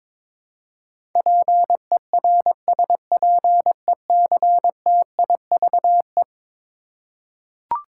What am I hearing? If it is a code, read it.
Morse code, heard as a keyed tone, about 22 wpm, 700 Hz. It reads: PERSPECTIVE
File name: practice / 22wpm / Eleven Letter Words - 22wpm